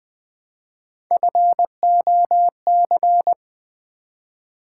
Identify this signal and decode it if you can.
Morse code, heard as a keyed tone, about 20 wpm, 700 Hz. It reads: FOC
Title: Build A Word - 1 Letter At A Time - 5-Letter Words - 20wpm